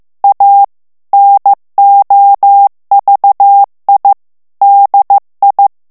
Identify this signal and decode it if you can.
Morse code, heard as a keyed tone, about 15 wpm, 800 Hz. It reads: A NOVI DI
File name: ITALIEN texte clair 1 vitesse 14 mots minute